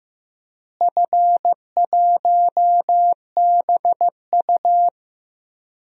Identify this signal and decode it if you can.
Morse code, heard as a keyed tone, about 15 wpm, 700 Hz. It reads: F1BU
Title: DXCC-100 Call Signs - 4 Characters - Repeated 3x - 15wpm